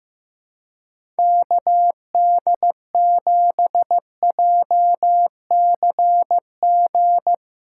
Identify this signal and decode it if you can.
Morse code, heard as a keyed tone, about 15 wpm, 700 Hz. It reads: KD7JCG
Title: USA Call Signs - 6 Characters - 4x Character Spacing - 15wpm